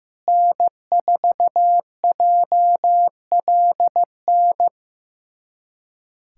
Morse code, heard as a keyed tone, about 15 wpm, 700 Hz. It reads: N4JLN
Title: USA Callsign DE Callsign - 5 Characters - 15wpm